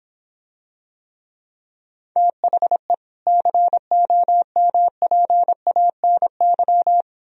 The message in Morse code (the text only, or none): THE COMPANY